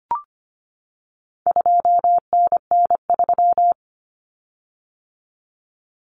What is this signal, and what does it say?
Morse code, heard as a keyed tone, about 25 wpm, 700 Hz. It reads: 2NN3
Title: Common Cut Numbers - 4-Digit 25wpm